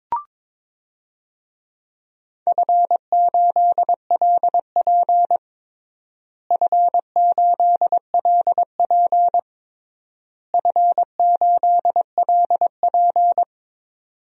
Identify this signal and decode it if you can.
Morse code, heard as a keyed tone, about 22 wpm, 700 Hz. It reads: F8LP F8LP F8LP
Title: DXCC-100 Call Signs - 4 Characters - Repeated 3x - 22wpm